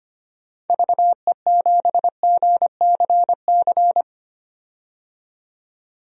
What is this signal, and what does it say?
Morse code, heard as a keyed tone, about 25 wpm, 700 Hz. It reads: VE7GCC
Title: DXCC-100 Call Signs - 6 Characters - 25wpm